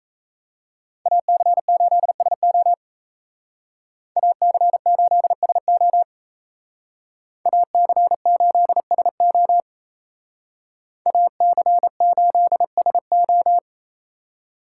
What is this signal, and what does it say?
Morse code, heard as a keyed tone, about 34 wpm, 700 Hz. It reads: AC8SO AC8SO AC8SO AC8SO
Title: Speed Racing - USA 2x2 Call Signs 28wpm